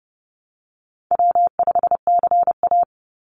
Morse code, heard as a keyed tone, about 30 wpm, 700 Hz. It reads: W5CA
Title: USA Call Signs 30wpm